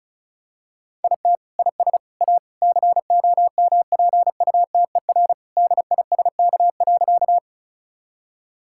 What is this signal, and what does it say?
Morse code, heard as a keyed tone, about 34 wpm, 700 Hz. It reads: IT IS A COMPUTER DISK.
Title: Sentences from Top 1000 Words - Review Part 6 - 35wpm